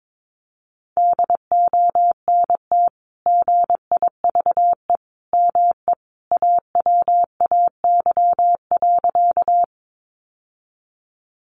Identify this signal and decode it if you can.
Morse code, heard as a keyed tone, about 22 wpm, 700 Hz. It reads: DONT GIVE ME AWAY.
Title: Sentences from Top 600 Words 22wpm